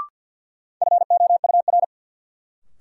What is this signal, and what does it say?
Morse code, heard as a keyed tone, about 47 wpm, 700 Hz. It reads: FOUR
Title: Sentences from Top 300 Words - New Word Drill - 50wpm